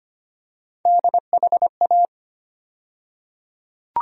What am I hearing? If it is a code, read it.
Morse code, heard as a keyed tone, about 25 wpm, 700 Hz. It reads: DHA
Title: USA Call Sign Postfixes - 3 Letters 25wpm